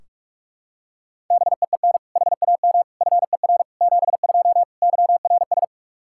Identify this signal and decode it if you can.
Morse code, heard as a keyed tone, about 41 wpm, 700 Hz. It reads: BEEN HAM FER 72 YRS
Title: Practice QSO - Exchange 3 - More Info - 45wpm